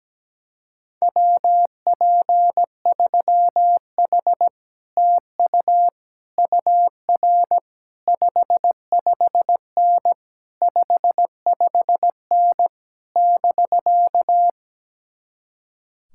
Morse code, heard as a keyed tone, about 17 wpm, 700 Hz. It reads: WP3H TU UR 55N 55N <BK>
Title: Speed-Racing Common QSO Phrases - 17wpm